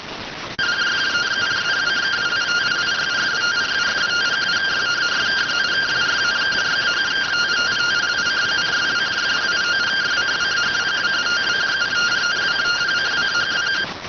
Two-tone teleprinter signal, about 170 Hz shift.